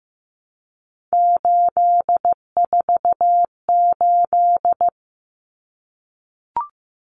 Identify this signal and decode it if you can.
Morse code, heard as a keyed tone, about 15 wpm, 700 Hz. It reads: 848
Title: Common Cut Numbers - 3-Digit 15wpm